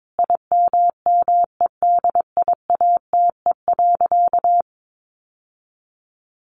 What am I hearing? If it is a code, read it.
Morse code, heard as a keyed tone, about 22 wpm, 700 Hz. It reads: IMMEDIATE.
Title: Sentences from Top 1200 Words 22wpm